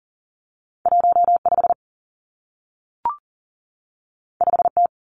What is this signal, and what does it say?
Morse code, heard as a keyed tone, about 38 wpm, 700 Hz. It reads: 15 5T
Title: Two Challenging Letter-Number Combinations 40wpm